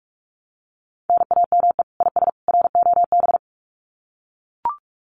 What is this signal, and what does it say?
Morse code, heard as a keyed tone, about 44 wpm, 700 Hz. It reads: NAME IS ROB
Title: Common QSO Phrases 45wpm